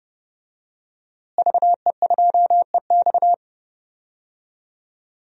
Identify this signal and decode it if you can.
Morse code, heard as a keyed tone, about 30 wpm, 700 Hz. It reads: VE2EX
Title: DXCC-100 Call Signs - 5 Characters - Repeated 2x - 30wpm